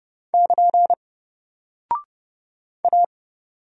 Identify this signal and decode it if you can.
Morse code, heard as a keyed tone, about 30 wpm, 700 Hz. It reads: <KN> A
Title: Single Character (Letter:Number:Punctuation:Prosign) Alt 30wpm